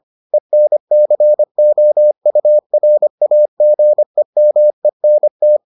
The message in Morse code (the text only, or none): ENCOURAGEMENT